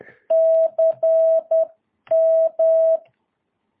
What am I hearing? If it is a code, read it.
Morse code, heard as a keyed tone, about 10 wpm, 650 Hz. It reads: CM
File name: jh8cbh / 2022cw